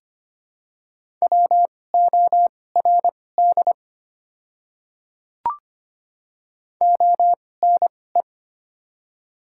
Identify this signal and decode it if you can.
Morse code, heard as a keyed tone, about 25 wpm, spaced as for 18 wpm, 700 Hz. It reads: WORD ONE
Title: 2x Character Spacing - Top 500 Words - 25wpm